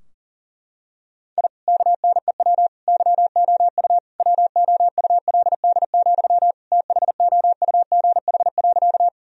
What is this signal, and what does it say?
Morse code, heard as a keyed tone, about 40 wpm, 700 Hz. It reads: I KNEW YOU WOULD, THOUGH.
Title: Sentences from Top 1000 Words - Review Part 2 - 40wpm